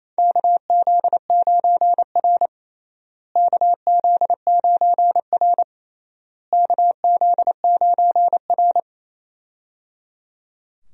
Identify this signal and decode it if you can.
Morse code, heard as a keyed tone, about 28 wpm, 700 Hz. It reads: KZ9R KZ9R KZ9R
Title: USA Call Signs - 4 Characters - Repeated 3x - 28wpm